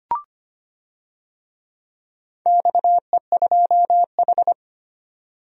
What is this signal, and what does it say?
Morse code, heard as a keyed tone, about 25 wpm, 700 Hz. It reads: XE2H
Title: DXCC-100 Call Signs - 4 Characters - 25wpm